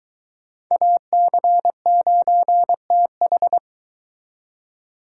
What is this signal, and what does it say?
Morse code, heard as a keyed tone, about 24 wpm, 700 Hz. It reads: AC9TH